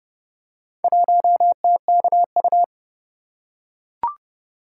Morse code, heard as a keyed tone, about 30 wpm, 700 Hz. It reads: 1TKU